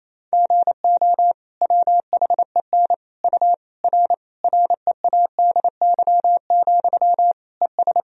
Morse code, heard as a keyed tone, about 28 wpm, 700 Hz. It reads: GO WHEN U R READY, ES
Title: Sentences from Top 500 Words and Abbreviations 28wpm